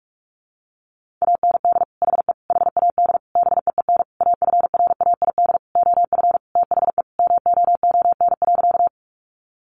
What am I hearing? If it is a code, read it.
Morse code, heard as a keyed tone, about 44 wpm, 700 Hz. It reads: AND HE HAD BEEN AFRAID OF THE MOON.
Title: Sentences from Top 1500 Words 45wpm